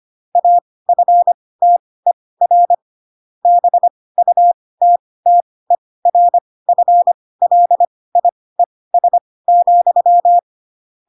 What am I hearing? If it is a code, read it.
Morse code, heard as a keyed tone, about 25 wpm, spaced as for 18 wpm, 700 Hz. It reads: AFTER BUTTERFLIES,
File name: Pinocchio / Pinocchio ch4 12 25 ik1hge